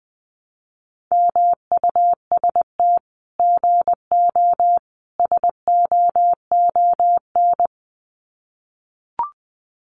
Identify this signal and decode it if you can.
Morse code, heard as a keyed tone, about 20 wpm, 700 Hz. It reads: MUST GO SOON
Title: Common QSO Phrases 20wpm